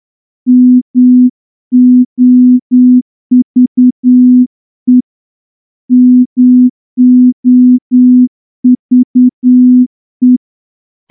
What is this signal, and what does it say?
Morse code, heard as a keyed tone, about 10 wpm, 250 Hz. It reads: MOVE MOVE
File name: morse-training-10WPM